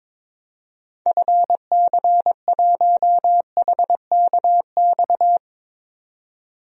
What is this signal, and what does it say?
Morse code, heard as a keyed tone, about 22 wpm, 700 Hz. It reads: FC1HKX
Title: DXCC-100 Call Signs - 6 Characters - 22wpm